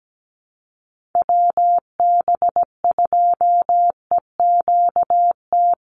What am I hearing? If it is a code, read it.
Morse code, heard as a keyed tone, about 17 wpm, 700 Hz. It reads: WB2EQT
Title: USA Call Signs 17wpm